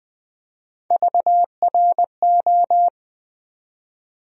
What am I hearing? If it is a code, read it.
Morse code, heard as a keyed tone, about 20 wpm, 700 Hz. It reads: VRO
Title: USA Call Sign Postfixes - 3 Letters 20wpm